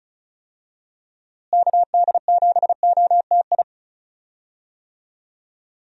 Morse code, heard as a keyed tone, about 36 wpm, 700 Hz. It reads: KD7OTI
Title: USA Call Signs - 6 Characters - Repeated 2x - 35wpm